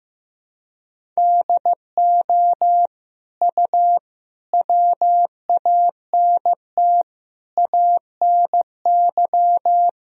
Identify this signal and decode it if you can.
Morse code, heard as a keyed tone, about 16 wpm, 700 Hz. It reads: DO U WANT ANY
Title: Sentences from Top 400 Words and Abbreviations 15wpm